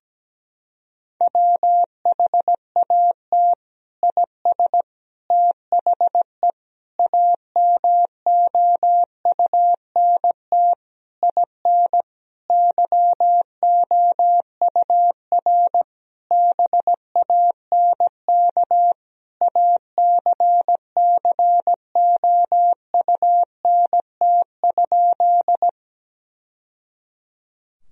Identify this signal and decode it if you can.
Morse code, heard as a keyed tone, about 17 wpm, 700 Hz. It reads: WHAT IS THE AMOUNT IN YOUR BANK ACCOUNT?
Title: Sentences from Top 1000 Words - Review Part 4 - 17wpm